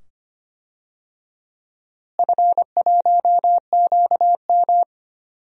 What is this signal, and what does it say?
Morse code, heard as a keyed tone, about 25 wpm, 700 Hz. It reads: F1QM
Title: DXCC-100 Call Signs - 4 Characters - Repeated 2x - 25wpm